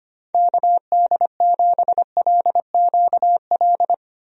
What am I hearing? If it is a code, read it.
Morse code, heard as a keyed tone, about 25 wpm, 700 Hz. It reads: KD7LQL
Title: USA Call Signs - 6 Characters - Repeated 2x - 25wpm